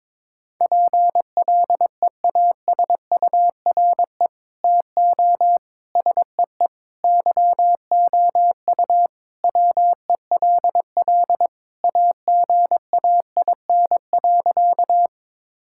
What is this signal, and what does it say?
Morse code, heard as a keyed tone, about 22 wpm, 700 Hz. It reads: PLEASURE TO SEE YOU WELL AGAIN.
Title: Sentences from Top 1600 Words 22wpm